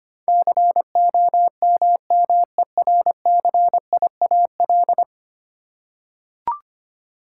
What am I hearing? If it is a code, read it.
Morse code, heard as a keyed tone, about 25 wpm, 700 Hz. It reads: COMMERCIAL
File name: 10 Letter Words - First 3 Letters Given - 25wpm